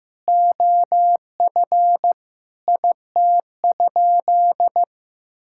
Morse code, heard as a keyed tone, about 16 wpm, 700 Hz. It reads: OF IT?
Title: Sentences from Top 1000 Words - Review Part 6 - 15wpm